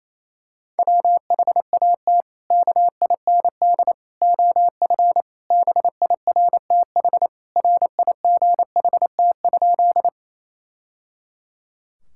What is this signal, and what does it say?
Morse code, heard as a keyed tone, about 28 wpm, 700 Hz. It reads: WHAT KIND OF BIRTH RIGHT?
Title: Sentences from Top 1000 Words - Review Part 5 - 28wpm